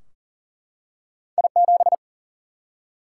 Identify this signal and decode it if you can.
Morse code, heard as a keyed tone, about 38 wpm, 700 Hz. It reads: I7
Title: Single-Letter Single-Number 40wpm